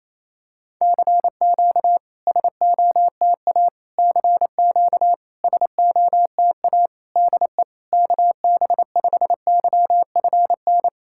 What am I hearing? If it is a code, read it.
Morse code, heard as a keyed tone, about 29 wpm, 660 Hz. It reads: CQ SOTA CQ SOTA DE KB5YFN
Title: SOTA QSO Practice 28wpm